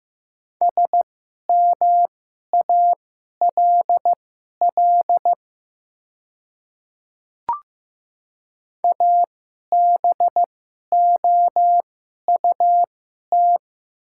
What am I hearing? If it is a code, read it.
Morse code, heard as a keyed tone, about 15 wpm, spaced as for 11 wpm, 700 Hz. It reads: SMALL ABOUT
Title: 2x Character Spacing - Top 300 Words - 15wpm